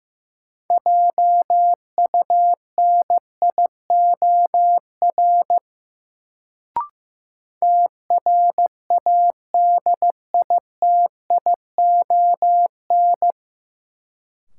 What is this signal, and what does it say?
Morse code, heard as a keyed tone, about 15 wpm, 700 Hz. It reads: JUNIOR TRADITION
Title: Sentences from Top 1500 Words - New Word Drill Alt - 15wpm